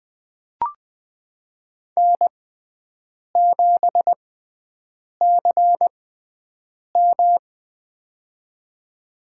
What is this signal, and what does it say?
Morse code, heard as a keyed tone, about 20 wpm, spaced as for 7 wpm, 700 Hz. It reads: N7CM
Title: USA Call Signs - 4 Characters - 6x Character Spacing - 20wpm